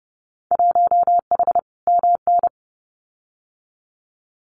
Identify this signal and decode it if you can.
Morse code, heard as a keyed tone, about 30 wpm, 700 Hz. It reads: 1H MN
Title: Winter Field Day Exchanges Encore 30wpm